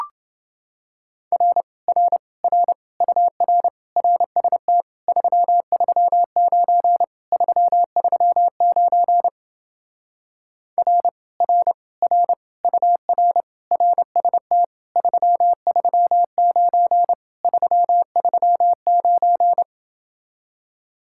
Morse code, heard as a keyed tone, about 28 wpm, 700 Hz. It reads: R R R UR RST 339 339 R R R UR RST 339 339